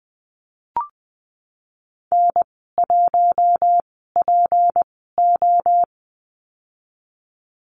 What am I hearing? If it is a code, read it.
Morse code, heard as a keyed tone, about 20 wpm, spaced as for 15 wpm, 700 Hz. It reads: N1PO
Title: 154 Lesson 25g - Letter p - Call Signs 20wpm